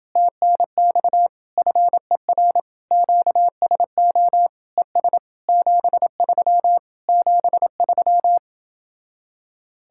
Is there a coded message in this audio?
Yes: Morse code from 0.2 to 8.4 s, heard as a keyed tone, about 29 wpm, 700 Hz. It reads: TNX FER QSO ES 73 73